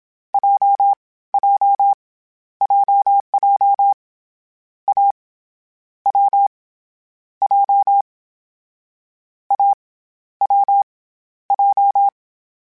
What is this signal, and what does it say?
Morse code, heard as a keyed tone, about 27 wpm, spaced as for 19 wpm, 800 Hz. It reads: JJ JJ A W J A W J
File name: Difficult Combos / Difficult Combos-mp3s / AWJ ShortRun@25